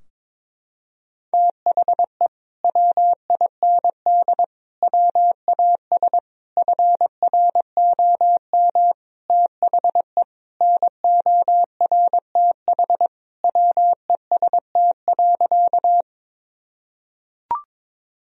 Morse code, heard as a keyed tone, about 22 wpm, 700 Hz. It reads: THE WIND WAS FROM THE NORTH WEST.
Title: Sentences from Top 1000 Words - Review Part 2 - 22wpm